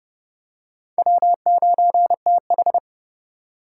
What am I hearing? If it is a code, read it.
Morse code, heard as a keyed tone, about 27 wpm, 700 Hz. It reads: W9TH